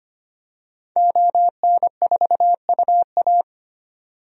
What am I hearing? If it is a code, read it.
Morse code, heard as a keyed tone, about 25 wpm, 700 Hz. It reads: ON4UA
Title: DXCC-100 Call Signs - 5 Characters - 25wpm